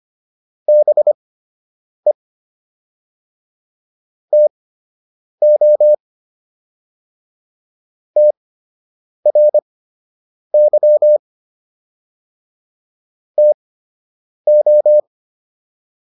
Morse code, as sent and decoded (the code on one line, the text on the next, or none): -... / - --- / - .-. -.-- / - ---
BE TO TRY TO